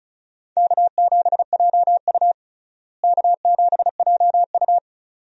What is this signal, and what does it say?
Morse code, heard as a keyed tone, about 36 wpm, 700 Hz. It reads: K7JU K7JU
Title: USA Call Signs - 4 Characters - Repeated 2x - 35wpm